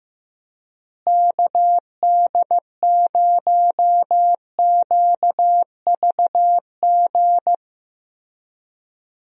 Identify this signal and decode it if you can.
Morse code, heard as a keyed tone, about 15 wpm, 700 Hz. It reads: KD0QVG
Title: USA Call Signs - 6 Characters - Repeated 3x - 15wpm